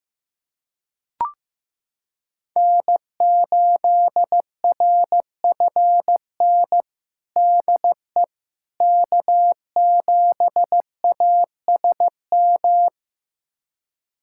Morse code, heard as a keyed tone, about 15 wpm, 700 Hz. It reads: N8RFN DE K7ASM
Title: USA Callsign DE Callsign - 5 Characters - 15wpm